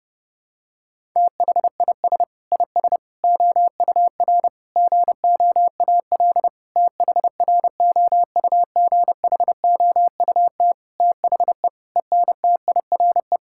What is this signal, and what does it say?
Morse code, heard as a keyed tone, about 30 wpm, 700 Hz. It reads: THIS IS OUR GOAL THROUGHOUT THE ENTIRE
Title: Sentences from Top 1000 Words - Review Part 5 - 30wpm